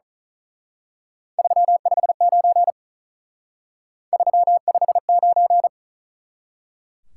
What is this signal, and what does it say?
Morse code, heard as a keyed tone, about 39 wpm, 700 Hz. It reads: 359 359
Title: Speed-Racing Top 180 QSO Elements 35wpm